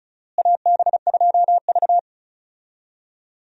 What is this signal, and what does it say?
Morse code, heard as a keyed tone, about 36 wpm, 700 Hz. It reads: AB2V
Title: USA Call Signs - 4 Characters - Repeated 2x - 35wpm